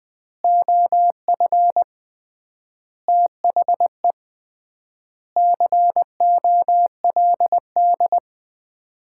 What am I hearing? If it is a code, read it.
Morse code, heard as a keyed tone, about 20 wpm, 700 Hz. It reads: OF THE COLD
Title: 3x Word-Spacing - Sets of 5 Words - 20wpm